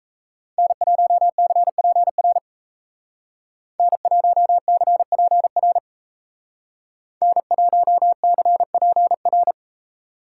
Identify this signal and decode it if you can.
Morse code, heard as a keyed tone, about 36 wpm, 700 Hz. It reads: N1CPR N1CPR N1CPR